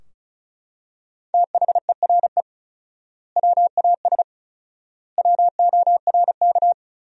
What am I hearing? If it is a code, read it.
Morse code, heard as a keyed tone, about 36 wpm, 700 Hz. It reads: THERE WAS WORK
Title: 4x Word-Spacing - Sentences from Top 200 Words - 35wpm